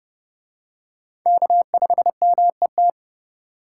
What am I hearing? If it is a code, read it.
Morse code, heard as a keyed tone, about 29 wpm, 700 Hz. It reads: K5MET